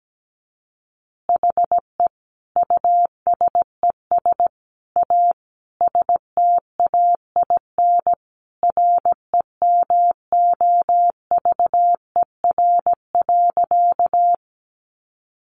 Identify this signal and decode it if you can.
Morse code, heard as a keyed tone, about 17 wpm, 700 Hz. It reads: HE USES A STAIN REMOVER.
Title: Additional Easy Phrases 17wpm